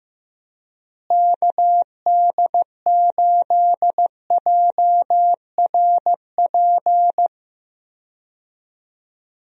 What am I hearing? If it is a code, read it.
Morse code, heard as a keyed tone, about 15 wpm, 700 Hz. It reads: KD8JRP
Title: USA Call Signs - 6 Characters - Repeated 3x - 15wpm